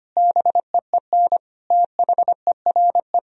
Morse code, heard as a keyed tone, about 25 wpm, 700 Hz. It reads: BEEN THERE
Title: Sentences from Top 1000 Words - Review Part 1 - 25wpm